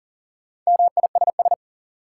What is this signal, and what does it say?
Morse code, heard as a keyed tone, about 40 wpm, 700 Hz. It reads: MISS
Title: Sentences from Top 500 Words - New Word Drill - 40wpm